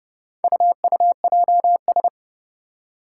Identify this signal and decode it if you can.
Morse code, heard as a keyed tone, about 30 wpm, 700 Hz. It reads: UUJS
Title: Four Challenging Letter-Number Combinations 30wpm